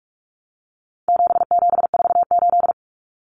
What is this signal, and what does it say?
Morse code, heard as a keyed tone, about 44 wpm, 700 Hz. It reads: 7748
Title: Common Cut Numbers - 4-Digit 45wpm